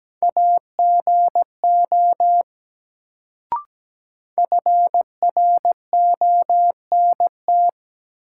Morse code, heard as a keyed tone, about 17 wpm, 700 Hz. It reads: AGO FRONT
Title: Sentences from Top 500 Words - New Word Drill - 17wpm